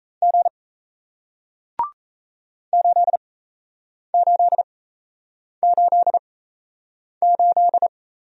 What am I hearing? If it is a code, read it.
Morse code, heard as a keyed tone, about 37 wpm, spaced as for 8 wpm, 700 Hz. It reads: G 8888